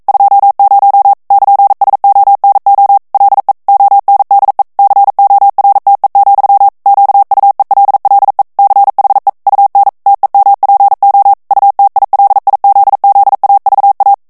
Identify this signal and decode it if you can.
Morse code, heard as a keyed tone, about 42 wpm, 800 Hz. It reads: 20 <KN>SONO LE ONDE CORTE, QUELLE CHE UN TEMPO UTILIZZAVA
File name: morse mp3 / ITALIEN texte clair 1 vitesse 40 mots minute